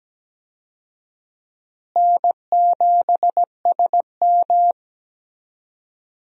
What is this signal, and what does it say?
Morse code, heard as a keyed tone, about 17 wpm, 700 Hz. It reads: N7SM
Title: USA Call Signs - 4 Characters - Repeated 2x - 17wpm